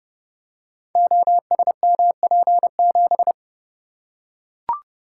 Morse code, heard as a keyed tone, about 30 wpm, 700 Hz. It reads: OSMP7
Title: Code Groups - 5 Letter:Numbers - 30wpm